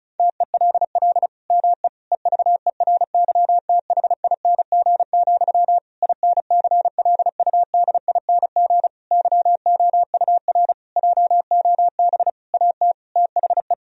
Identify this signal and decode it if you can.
Morse code, heard as a keyed tone, about 34 wpm, 700 Hz. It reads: TELL ME EVERYTHING, INCLUDING YOUR JOB AT THE
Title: Sentences from Top 1000 Words - Review Part 4 - 35wpm